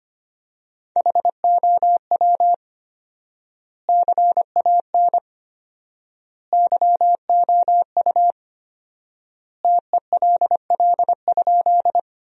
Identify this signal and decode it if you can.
Morse code, heard as a keyed tone, about 25 wpm, 700 Hz. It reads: HOW CAN YOU TELL?
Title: 4x Word-Spacing - Sentences from Top 200 Words - 25wpm